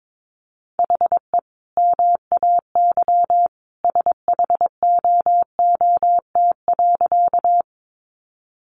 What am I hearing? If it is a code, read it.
Morse code, heard as a keyed tone, about 22 wpm, 700 Hz. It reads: HE MAY SHOOT.
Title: Sentences from Top 1200 Words 22wpm